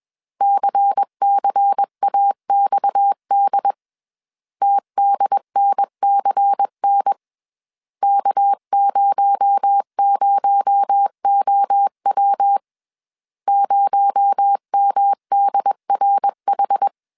Morse code, heard as a keyed tone, about 20 wpm, 800 Hz. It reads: CCAXD TDNCN K00OW 0MDRH
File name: CW / DET / 1 6 1500C S C 20wpm 2116